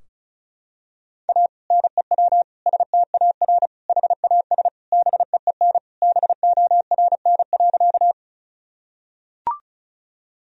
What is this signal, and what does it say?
Morse code, heard as a keyed tone, about 36 wpm, 700 Hz. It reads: A NEW STAR HAS BEEN BORN.
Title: Sentences from Top 1000 Words - Review Part 2 - 35wpm